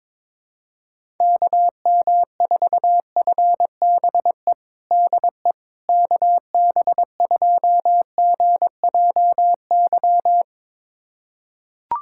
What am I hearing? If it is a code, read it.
Morse code, heard as a keyed tone, about 21 wpm, 700 Hz. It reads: KM4FBE DE KB2GJY
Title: USA Callsign DE Callsign - 6 Characters - 22wpm